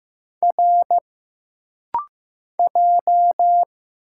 Morse code, heard as a keyed tone, about 15 wpm, 700 Hz. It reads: R J
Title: Single Letters - Rapid-Fire 15wpm